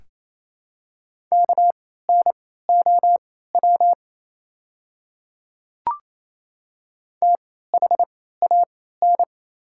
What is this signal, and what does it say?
Morse code, heard as a keyed tone, about 28 wpm, spaced as for 16 wpm, 700 Hz. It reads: KNOW THAN
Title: 3x Character Spacing - Top 100 Words - 28wpm